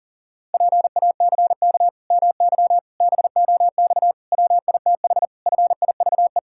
Morse code, heard as a keyed tone, about 40 wpm, 700 Hz. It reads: PACK MY BOX WITH FIVE
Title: Pangram Challenge v2 - 40wpm